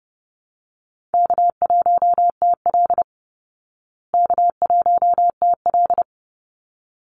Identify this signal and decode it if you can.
Morse code, heard as a keyed tone, about 30 wpm, 700 Hz. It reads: K1TL K1TL
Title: Easy Call Signs Repeated 2x - 30wpm